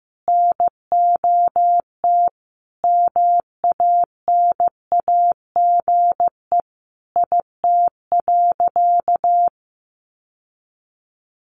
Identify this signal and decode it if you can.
Morse code, heard as a keyed tone, about 15 wpm, 700 Hz. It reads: NOT MANAGE IT.
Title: Sentences from Top 1400 Words 15wpm